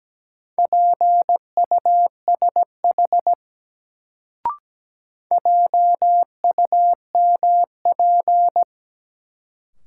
Morse code, heard as a keyed tone, about 17 wpm, 700 Hz. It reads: PUSH JUMP
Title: Sentences from Top 1300 Words - New Word Drill - 17wpm